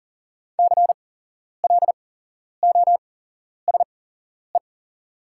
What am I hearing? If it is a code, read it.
Morse code, heard as a keyed tone, about 38 wpm, spaced as for 11 wpm, 700 Hz. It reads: CLOSE
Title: 8x Character Spacing - Top 300 Words - 40wpm